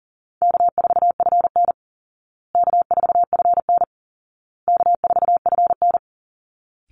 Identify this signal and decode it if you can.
Morse code, heard as a keyed tone, about 40 wpm, 700 Hz. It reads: K4FN K4FN K4FN
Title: Call Signs Repeated 2x - 40wpm